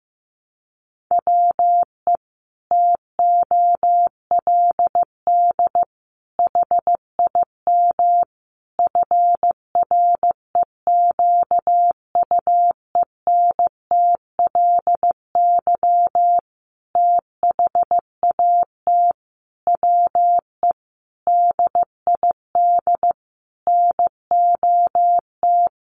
Morse code, heard as a keyed tone, about 15 wpm, 700 Hz. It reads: WE TOLD HIM FREQUENTLY THAT WE DID NOT
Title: Sentences from Top 1300 Words 15wpm